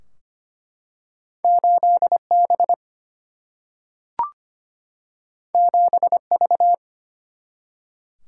Morse code, heard as a keyed tone, about 25 wpm, 700 Hz. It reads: 8 B 7 V
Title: Single-Letter Single-Number 25wpm